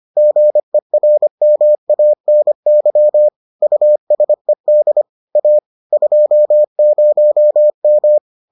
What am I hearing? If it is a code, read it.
Morse code, heard as a keyed tone, about 25 wpm, 600 Hz. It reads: GERMANY USED A 20M